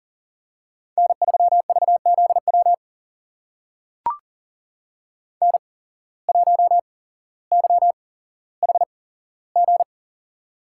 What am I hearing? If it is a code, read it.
Morse code, heard as a keyed tone, about 40 wpm, spaced as for 17 wpm, 700 Hz. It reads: N3VZW N1YHG
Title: USA Call Signs - 5 Characters - 8x Character Spacing - 40wpm